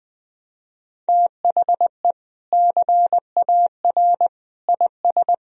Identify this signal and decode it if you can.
Morse code, heard as a keyed tone, about 20 wpm, 700 Hz. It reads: THE CAR IS
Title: Sentences from Top 1000 Words - Review Part 1 - 20wpm